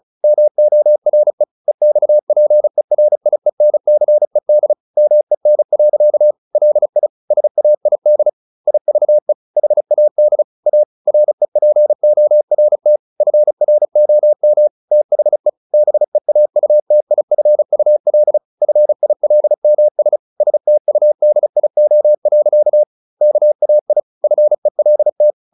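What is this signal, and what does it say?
Morse code, heard as a keyed tone, about 34 wpm, 600 Hz. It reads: MORE EXPERIENCED MEN. LI SAID IVE HAD A REPORT FROM THE BEAUTIFUL FILMS STUDIO. KAI FELT
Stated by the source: Morse practice at 35 wpm from CW med Gnister, sending Ken Follett's novel Never in English.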